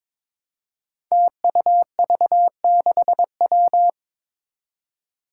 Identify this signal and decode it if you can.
Morse code, heard as a keyed tone, about 21 wpm, 700 Hz. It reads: TUV6W
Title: Five Challenging Letter-Number Combinations 22wpm